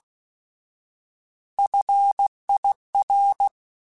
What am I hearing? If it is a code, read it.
Morse code, heard as a keyed tone, about 16 wpm, 800 Hz. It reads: FIR
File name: Alphabetical / Alphabetical mp3s / Fs @15